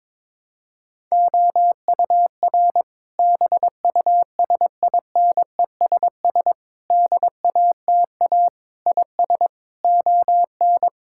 Morse code, heard as a keyed tone, about 22 wpm, 700 Hz. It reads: OUR BUSINESS DATA IS ON
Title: Sentences from Top 1000 Words - Review Part 6 - 22wpm